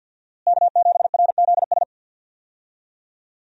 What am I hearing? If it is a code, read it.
Morse code, heard as a keyed tone, about 52 wpm, 700 Hz. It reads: K7RZS
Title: USA Call Signs - 5 Characters - Repeated 3x - 50wpm